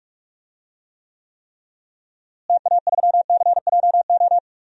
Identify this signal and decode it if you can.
Morse code, heard as a keyed tone, about 46 wpm, 700 Hz. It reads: TA3CJO
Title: DXCC-100 Call Signs - 6 Characters - Repeated 3x - 45wpm